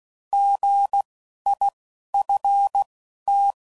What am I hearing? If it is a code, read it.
Morse code, heard as a keyed tone, about 16 wpm, spaced as for 11 wpm, 800 Hz. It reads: GIFT